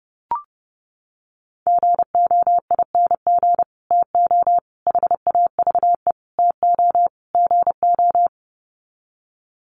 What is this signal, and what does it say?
Morse code, heard as a keyed tone, about 30 wpm, 700 Hz. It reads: GOING TO HAVE TO GO
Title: Sets of 5 Words Encore 30wpm